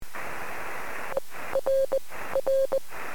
Morse code, heard as a keyed tone, about 19 wpm, spaced as for 14 wpm, 530 Hz, with noise. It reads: ERR
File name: sm7lcb / 1800 sk7l 23cm